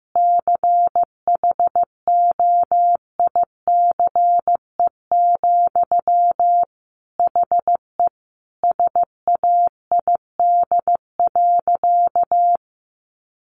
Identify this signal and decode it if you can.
Morse code, heard as a keyed tone, about 15 wpm, 700 Hz. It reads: CHOICE, HE SAID.
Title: Sentences from Top 900 Words 15wpm